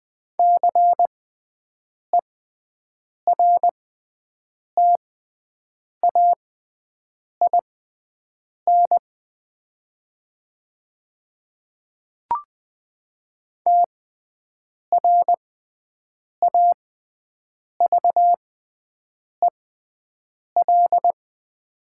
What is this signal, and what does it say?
Morse code, heard as a keyed tone, about 20 wpm, spaced as for 7 wpm, 700 Hz. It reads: CERTAIN TRAVEL
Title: 6x Character Spacing - Top 500 Words - 20wpm